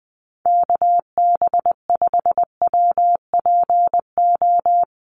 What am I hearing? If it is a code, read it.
Morse code, heard as a keyed tone, about 20 wpm, 700 Hz. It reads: KB5WPO
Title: USA Call Signs Repeated 2x 20wpm